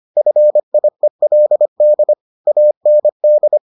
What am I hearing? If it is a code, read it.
Morse code, heard as a keyed tone, about 25 wpm, 600 Hz. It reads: FIELD AND